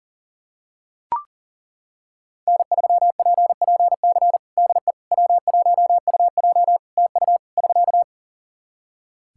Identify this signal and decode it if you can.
Morse code, heard as a keyed tone, about 40 wpm, 700 Hz. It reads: N3PPC DE W1UJ TU <SK>
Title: Practice QSO - Exchange 4 - Ending - 40wpm